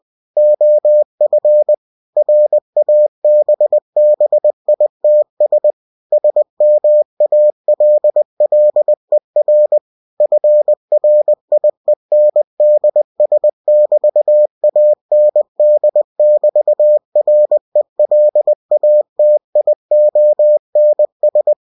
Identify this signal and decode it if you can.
Morse code, heard as a keyed tone, about 20 wpm, 600 Hz. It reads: OF RABBITS SMALLER FRIENDS<BT>AND<BT>RELATIONS
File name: cw-books / winnie-the-pooh / winnie-20wpm-page-0079